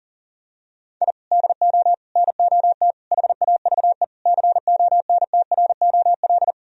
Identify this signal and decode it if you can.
Morse code, heard as a keyed tone, about 40 wpm, 700 Hz. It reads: I DO NOT HAVE CONTROL
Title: Sentences from Top 1000 Words - Review Part 3 - 40wpm